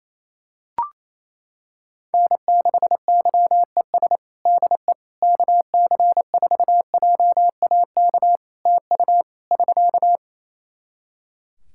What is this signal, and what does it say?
Morse code, heard as a keyed tone, about 27 wpm, 700 Hz. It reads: N6YES DE KC4JAK TU <SK>
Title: K1USN SST QSO Practice 28wpm